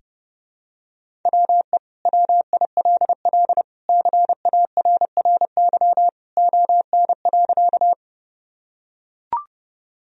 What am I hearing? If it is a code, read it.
Morse code, heard as a keyed tone, about 30 wpm, 700 Hz. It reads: WE WILL CARRY ON.
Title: Sentences from Top 1000 Words - Review Part 1 - 30wpm